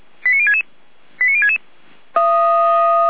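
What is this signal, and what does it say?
Five-tone selective call (ZVEI), heard as quick runs of five tones, tones of about 70 ms. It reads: 89066 89066
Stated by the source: ZVEI five-tone selective call received by SDR.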